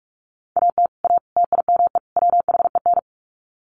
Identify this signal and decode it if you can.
Morse code, heard as a keyed tone, about 44 wpm, 700 Hz. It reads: AT A TIME WHEN
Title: Sets of 4 Words 45wpm